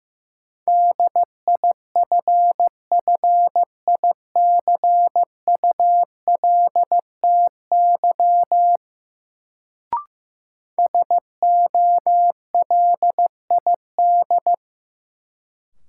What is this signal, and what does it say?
Morse code, heard as a keyed tone, about 15 wpm, 700 Hz. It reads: DIFFICULTY SOLID
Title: Sentences from Top 1500 Words - New Word Drill Alt - 15wpm